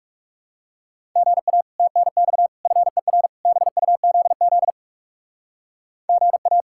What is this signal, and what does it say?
Morse code, heard as a keyed tone, about 37 wpm, 700 Hz. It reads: GA TNX FER BUZZ GA
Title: Speed-Racing Common QSO Phrases - 30wpm